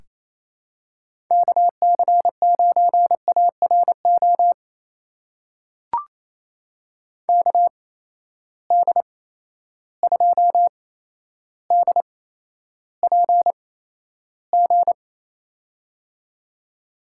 Morse code, heard as a keyed tone, about 28 wpm, spaced as for 12 wpm, 700 Hz. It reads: KC9ARO KD2DPG
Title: USA Call Signs - 6 Characters - 8x Character Spacing - 28wpm